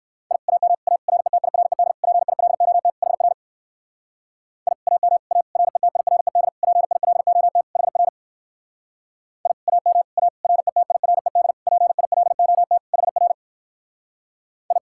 Morse code, heard as a keyed tone, about 57 wpm, 700 Hz. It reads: I AM A RETIRED PILOT HR I AM A RETIRED PILOT HR I AM A RETIRED PILOT HR I